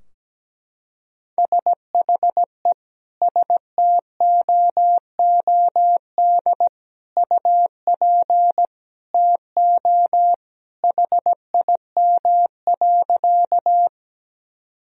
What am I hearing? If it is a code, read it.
Morse code, heard as a keyed tone, about 17 wpm, 700 Hz. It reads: SHE STOOD UP TO HIM.
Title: Sentences from Top 1000 Words - Review Part 2 - 17wpm